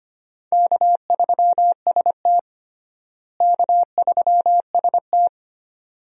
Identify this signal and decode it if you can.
Morse code, heard as a keyed tone, about 25 wpm, 700 Hz. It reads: K3ST K3ST
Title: USA Call Signs - 4 Characters - Repeated 2x - 25wpm